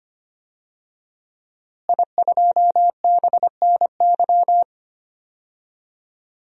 Morse code, heard as a keyed tone, about 25 wpm, 700 Hz. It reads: I2BNY
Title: DXCC-100 Call Signs - 5 Characters - Repeated 3x - 25wpm